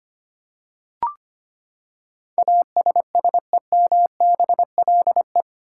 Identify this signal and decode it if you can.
Morse code, heard as a keyed tone, about 25 wpm, 700 Hz. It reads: ASSEMBLE
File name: Eight Letter Words - 25wpm